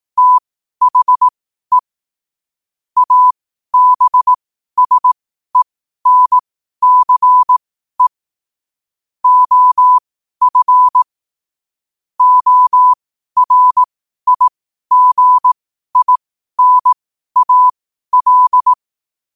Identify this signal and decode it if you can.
Morse code, heard as a keyed tone, about 18 wpm, spaced as for 12 wpm, 1000 Hz. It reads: THE ABSENCE OF ORIGINAL